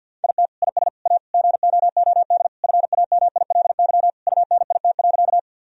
Morse code, heard as a keyed tone, about 51 wpm, 700 Hz. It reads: IT IS A GOOD FAMILY UNIT.
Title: Sentences from Top 1000 Words - Review Part 2 - 50wpm